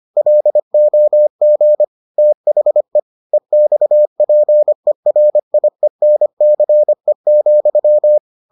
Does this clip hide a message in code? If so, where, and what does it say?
Yes: Morse code from 0.2 to 8.2 s, heard as a keyed tone, about 25 wpm, 600 Hz. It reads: LOG THE EXPERIENCE,